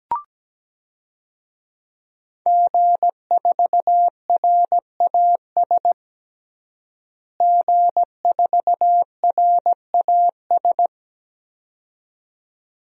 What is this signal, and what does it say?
Morse code, heard as a keyed tone, about 17 wpm, 700 Hz. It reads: G4RAS G4RAS
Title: DXCC-100 Call Signs - 5 Characters - Repeated 2x - 17wpm